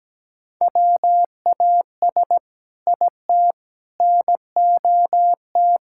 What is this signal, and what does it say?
Morse code, heard as a keyed tone, about 17 wpm, 700 Hz. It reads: WAS IT NOT
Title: Sentences from Top 1600 Words 17wpm